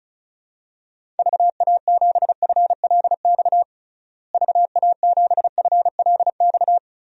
Morse code, heard as a keyed tone, about 35 wpm, 700 Hz. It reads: VA7FLX VA7FLX
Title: DXCC-100 Call Signs - 6 Characters - Repeated 2x - 35wpm